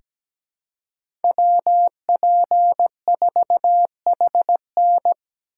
Morse code, heard as a keyed tone, about 17 wpm, 700 Hz. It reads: WP4HN
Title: USA Call Signs - 5 Characters - 2x Character Spacing - 17wpm